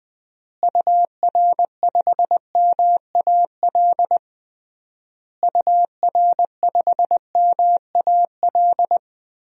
Morse code, heard as a keyed tone, about 20 wpm, 700 Hz. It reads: UR5MAL UR5MAL
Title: DXCC-100 Call Signs - 6 Characters - Repeated 3x - 20wpm